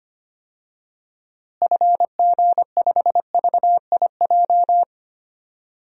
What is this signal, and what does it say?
Morse code, heard as a keyed tone, about 25 wpm, 700 Hz. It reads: FG5VIJ
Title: DXCC-100 Call Signs - 6 Characters - Repeated 2x - 25wpm